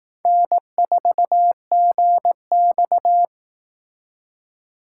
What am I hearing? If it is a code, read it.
Morse code, heard as a keyed tone, about 18 wpm, 700 Hz. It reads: N4GX
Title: Speed Racing - USA 1x2 Call Signs 15wpm